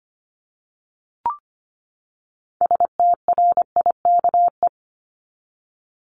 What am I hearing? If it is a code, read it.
Morse code, heard as a keyed tone, about 25 wpm, 700 Hz. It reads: STRIKE
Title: Top 1501-1750 Words 25wpm